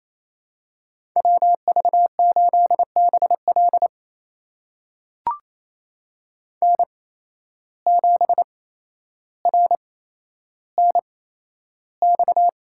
Morse code, heard as a keyed tone, about 28 wpm, spaced as for 12 wpm, 700 Hz. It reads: WV8BL N7RNX
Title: USA Call Signs - 5 Characters - 8x Character Spacing - 28wpm